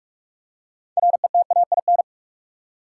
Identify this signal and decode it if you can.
Morse code, heard as a keyed tone, about 47 wpm, 700 Hz. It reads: RETAIN
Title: Six Letter Words - 45wpm